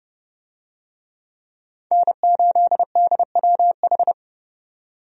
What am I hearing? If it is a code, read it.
Morse code, heard as a keyed tone, about 30 wpm, 700 Hz. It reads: N8DWH
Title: USA Call Signs - 5 Characters - 30wpm